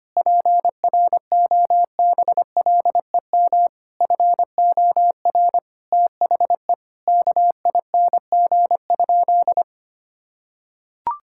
Morse code, heard as a keyed tone, about 25 wpm, 700 Hz. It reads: PROBLEM FOR THE KING?
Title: Sentences from Top 1000 Words - Review Part 5 - 25wpm